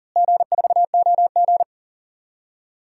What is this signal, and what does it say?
Morse code, heard as a keyed tone, about 40 wpm, 700 Hz. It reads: G4OG
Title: DXCC-100 Call Signs - 4 Characters - Repeated 3x - 40wpm